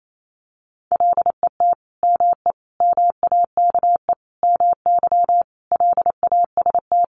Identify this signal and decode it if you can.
Morse code, heard as a keyed tone, about 28 wpm, 700 Hz. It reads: LET ME MAKE MY LAST
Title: Sentences from Top 1300 Words 28wpm